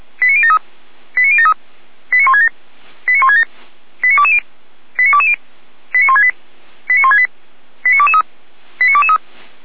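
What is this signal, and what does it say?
Five-tone selective call (ZVEI), heard as quick runs of five tones, tones of about 70 ms. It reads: 89073 89073 89167 89167 89220 89220 89168 89168 89223 89223
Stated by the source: ZVEI five-tone selective call received on SDR.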